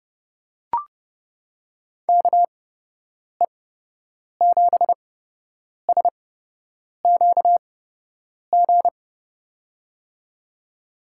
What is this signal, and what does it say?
Morse code, heard as a keyed tone, about 31 wpm, spaced as for 8 wpm, 700 Hz. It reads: KE7SQG
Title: USA Call Signs - 6 Characters - 8x Character Spacing - 30wpm